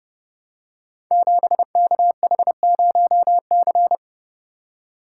Morse code, heard as a keyed tone, about 30 wpm, 700 Hz. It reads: 7KH0C
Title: Code Groups - 5 Letter:Numbers - 30wpm